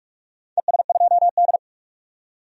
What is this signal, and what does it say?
Morse code, heard as a keyed tone, about 44 wpm, 700 Hz. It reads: ES2D
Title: DXCC-100 Call Signs - 4 Characters - 45wpm